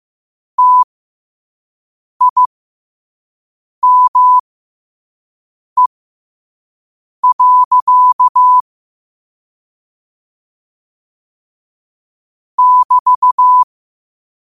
Morse code, heard as a keyed tone, about 15 wpm, spaced as for 5 wpm, 1000 Hz. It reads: TIME. <BT>
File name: qotdfiles / 5wpm